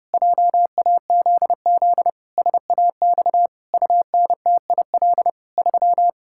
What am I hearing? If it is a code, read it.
Morse code, heard as a keyed tone, about 28 wpm, 700 Hz. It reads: JAZZ SAX UNTIL 3
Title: Pangram Challenge v2 - 30wpm